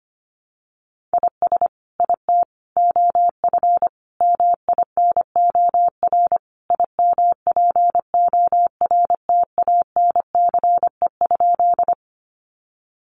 Morse code, heard as a keyed tone, about 25 wpm, 700 Hz. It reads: IS IT OF MINOR IMPORTANCE?
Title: Sentences from Top 1500 Words 25wpm